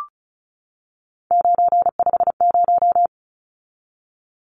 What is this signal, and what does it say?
Morse code, heard as a keyed tone, about 34 wpm, 700 Hz. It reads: 950
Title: Numbers - 3-Digit 35wpm